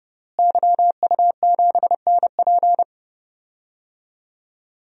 Morse code, heard as a keyed tone, about 30 wpm, 700 Hz. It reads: YU7NP
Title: DXCC-100 Call Signs - 5 Characters - Repeated 2x - 30wpm